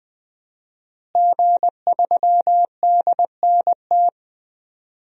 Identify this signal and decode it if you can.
Morse code, heard as a keyed tone, about 20 wpm, 700 Hz. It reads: G3DNT
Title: DXCC-100 Call Signs - 5 Characters - Repeated 3x - 20wpm